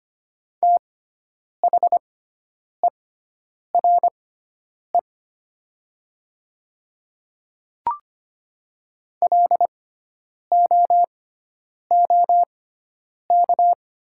Morse code, heard as a keyed tone, about 25 wpm, spaced as for 9 wpm, 700 Hz. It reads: THERE LOOK
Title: 6x Character Spacing - Top 300 Words - 25wpm